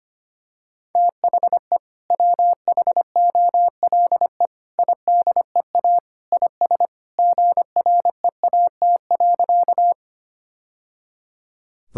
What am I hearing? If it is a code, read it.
Morse code, heard as a keyed tone, about 25 wpm, 690 Hz. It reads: THE WHOLE IDEA IS GREAT.
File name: practice / 25wpm / Sentences with 2-Word Phrases - 25wpm